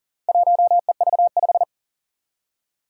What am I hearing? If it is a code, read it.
Morse code, heard as a keyed tone, about 40 wpm, 700 Hz. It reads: 1EV5
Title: Four Challenging Letter-Number Combinations 40wpm